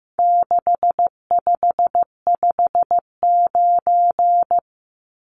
Morse code, heard as a keyed tone, about 15 wpm, 700 Hz. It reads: 6559
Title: Numbers - 4-Digit 15wpm